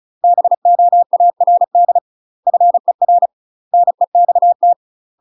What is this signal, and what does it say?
Morse code, heard as a keyed tone, about 34 wpm, 700 Hz. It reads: BOARD FER NEXT
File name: UK MASTER QSOs TRAINING / 35WPM / UK MASTER QSOs 0015